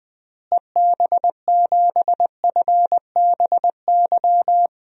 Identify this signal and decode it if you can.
Morse code, heard as a keyed tone, about 20 wpm, 700 Hz. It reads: EB7FBY
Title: DXCC-100 Call Signs - 6 Characters - Repeated 3x - 20wpm